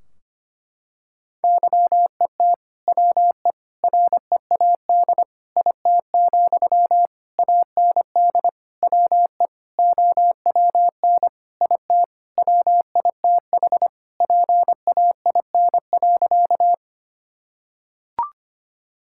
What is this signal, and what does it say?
Morse code, heard as a keyed tone, about 25 wpm, 700 Hz. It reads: YET WE READ IT, AND WE OWN IT WITH PAIN.
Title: Sentences from Top 1600 Words 25wpm